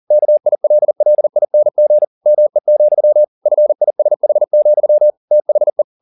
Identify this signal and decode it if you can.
Morse code, heard as a keyed tone, about 40 wpm, 600 Hz. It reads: KILLING ME, FISH, THE
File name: the old man and the sea 2 40wpm